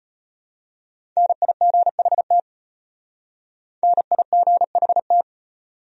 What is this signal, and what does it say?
Morse code, heard as a keyed tone, about 32 wpm, 700 Hz. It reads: NIGHT NIGHT